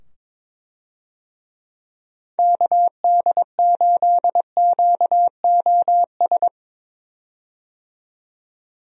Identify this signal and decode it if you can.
Morse code, heard as a keyed tone, about 22 wpm, 700 Hz. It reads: KD8QOS
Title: USA Call Signs - 6 Characters - 22wpm